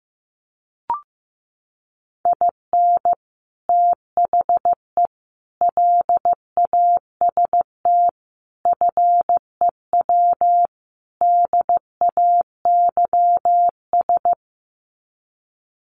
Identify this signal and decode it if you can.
Morse code, heard as a keyed tone, about 15 wpm, 700 Hz. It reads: IN THE LAST FEW DAYS
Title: Sets of 5 Words Encore2 15wpm